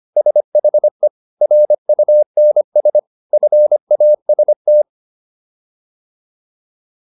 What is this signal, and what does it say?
Morse code, heard as a keyed tone, about 25 wpm, 600 Hz. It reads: SHE RUNS FAST